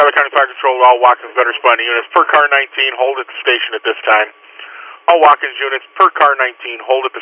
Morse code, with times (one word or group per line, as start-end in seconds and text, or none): none